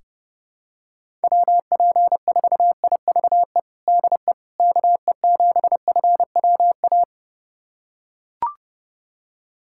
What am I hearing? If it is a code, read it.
Morse code, heard as a keyed tone, about 30 wpm, 700 Hz. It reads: WP4IVE DE KE7FWA
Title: USA Callsign DE Callsign 30wpm